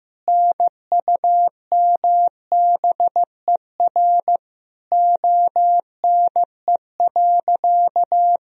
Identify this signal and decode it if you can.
Morse code, heard as a keyed tone, about 15 wpm, 700 Hz. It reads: NUMBER ONE.
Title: Sentences from Top 1000 Words - Review Part 1 - 15wpm